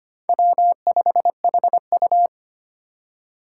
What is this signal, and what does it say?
Morse code, heard as a keyed tone, about 29 wpm, 700 Hz. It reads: W5HU
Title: Speed Racing - USA 1x2 Call Signs 25wpm